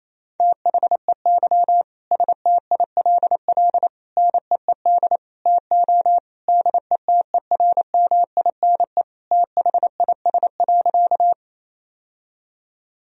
Morse code, heard as a keyed tone, about 28 wpm, 700 Hz. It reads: THEY STILL NEED TO DETERMINE THIS.
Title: Sentences from Top 1000 Words - Review Part 4 - 28wpm